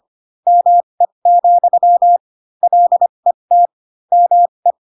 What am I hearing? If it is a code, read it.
Morse code, heard as a keyed tone, about 25 wpm, spaced as for 22 wpm, 700 Hz. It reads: ME, LET ME
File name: Pinocchio / Pinocchio ch4 18 25 ik1hge